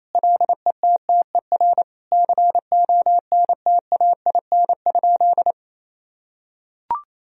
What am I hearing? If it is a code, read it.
Morse code, heard as a keyed tone, about 28 wpm, 700 Hz. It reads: LETTER CONTAIN?
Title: Sentences from Top 1000 Words - Review Part 2 - 28wpm